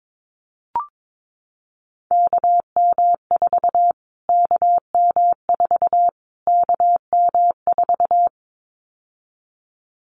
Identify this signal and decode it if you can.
Morse code, heard as a keyed tone, about 22 wpm, 700 Hz. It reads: KM4 KM4 KM4
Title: Callsign Prefixes for Top 25 DXCC - Repeated 3x - 22wpm